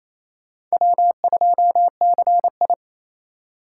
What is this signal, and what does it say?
Morse code, heard as a keyed tone, about 36 wpm, 700 Hz. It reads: W2CI